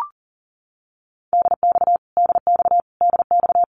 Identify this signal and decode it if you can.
Morse code, heard as a keyed tone, about 38 wpm, 700 Hz. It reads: DX DX DX
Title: Abbreviations Repeated 3x - 40wpm